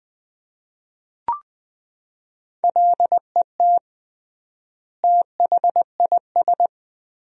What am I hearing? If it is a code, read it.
Morse code, heard as a keyed tone, about 20 wpm, 700 Hz. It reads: LET THIS
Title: 3x Word-Spacing - Sentences from Top 400 Words - 20wpm